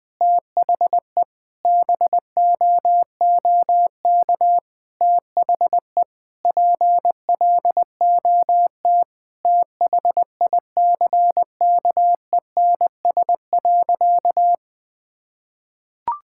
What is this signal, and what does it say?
Morse code, heard as a keyed tone, about 21 wpm, 700 Hz. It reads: THE BOOK THE PLOT THICKENS.
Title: Sentences with 5-Word Phrases - 20wpm